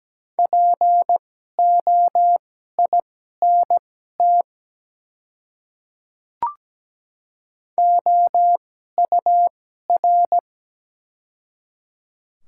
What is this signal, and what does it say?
Morse code, heard as a keyed tone, about 17 wpm, spaced as for 12 wpm, 700 Hz. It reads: POINT OUR
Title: 2x Character Spacing - Top 200 Words - 17wpm